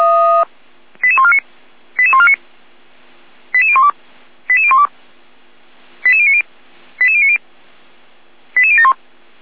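Five-tone selective call (ZVEI), heard as quick runs of five tones, tones of about 70 ms. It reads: 88149 88149 88012 88012 88090 88090 88071
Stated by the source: ZVEI five-tone call received by SDR.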